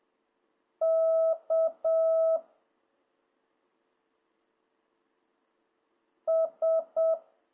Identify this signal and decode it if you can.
Morse code, heard as a keyed tone, about 7 wpm, 650 Hz. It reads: K S